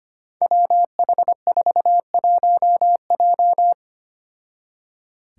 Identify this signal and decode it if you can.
Morse code, heard as a keyed tone, about 25 wpm, 700 Hz. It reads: WH41J
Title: Five Challenging Letter-Number Combinations 25wpm